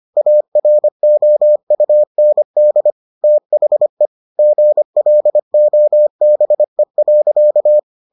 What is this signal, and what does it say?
Morse code, heard as a keyed tone, about 25 wpm, 600 Hz. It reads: AROUND THE GLOBE.